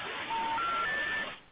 Special information tone, with noise.